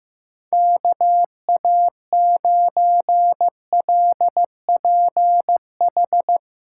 Morse code, heard as a keyed tone, about 15 wpm, 700 Hz. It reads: KA9LPH
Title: USA Call Signs - 6 Characters - 8x Character Spacing - 15wpm